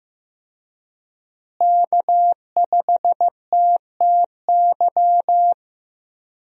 Morse code, heard as a keyed tone, about 15 wpm, 700 Hz. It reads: K5TTY
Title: USA Call Signs - 5 Characters - Repeated 3x - 15wpm